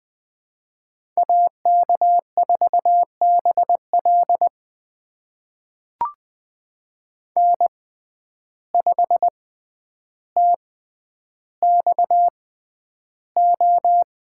Morse code, heard as a keyed tone, about 20 wpm, spaced as for 10 wpm, 700 Hz. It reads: AK4BL N5TXO
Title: USA Call Signs - 5 Characters - 6x Character Spacing - 20wpm